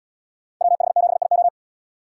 Morse code, heard as a keyed tone, about 77 wpm, 690 Hz. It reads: WH7IO